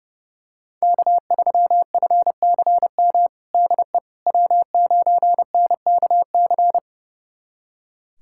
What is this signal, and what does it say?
Morse code, heard as a keyed tone, about 32 wpm, 700 Hz. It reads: K3FCM DE W9NKC
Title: USA Callsign DE Callsign - 5 Characters - 30wpm